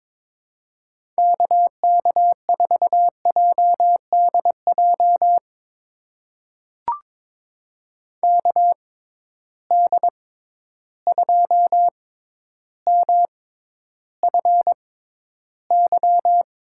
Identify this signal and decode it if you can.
Morse code, heard as a keyed tone, about 23 wpm, spaced as for 11 wpm, 700 Hz. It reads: KK4JDJ KD2MFY
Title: USA Call Signs - 6 Characters - 6x Character Spacing - 22wpm